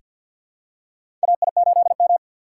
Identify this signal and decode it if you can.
Morse code, heard as a keyed tone, about 48 wpm, 700 Hz. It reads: AI8M
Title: USA Call Signs - 3x Character Spacing - 50wpm